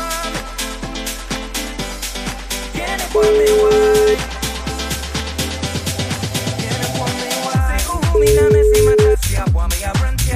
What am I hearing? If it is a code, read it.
Ringback tone (Europe, 425 Hz), with noise.